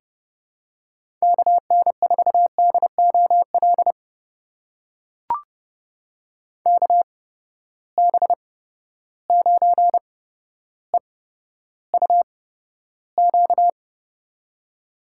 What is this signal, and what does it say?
Morse code, heard as a keyed tone, about 30 wpm, spaced as for 13 wpm, 700 Hz. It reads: KN4DOL KB9EUQ
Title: USA Call Signs - 6 Characters - 8x Character Spacing - 30wpm